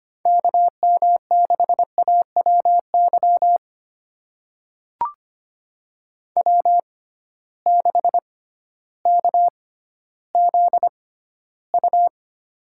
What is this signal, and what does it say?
Morse code, heard as a keyed tone, about 25 wpm, spaced as for 13 wpm, 700 Hz. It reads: KM6AWY W6KZU
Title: USA Call Signs - 6x Character Spacing - 25wpm